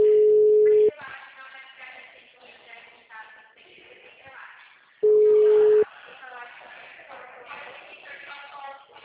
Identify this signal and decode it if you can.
Ringback tone (Europe, 425 Hz), with speech.
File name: out- -2041- - - .12730